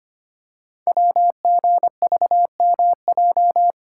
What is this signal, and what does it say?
Morse code, heard as a keyed tone, about 25 wpm, 690 Hz. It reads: WGVMJ
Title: Code Groups - 5 Letters - 25wpm